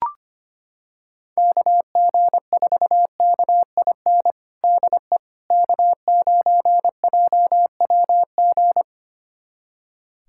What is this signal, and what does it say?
Morse code, heard as a keyed tone, about 25 wpm, 700 Hz. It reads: KG4KIN DE K9JWG
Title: USA Callsign DE Callsign 25wpm